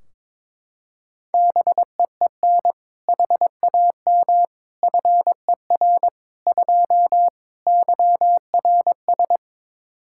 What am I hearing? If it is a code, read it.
Morse code, heard as a keyed tone, about 22 wpm, 700 Hz. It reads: BEEN HAM FER 2 YRS
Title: Practice QSO - Exchange 3 - More Info - 22wpm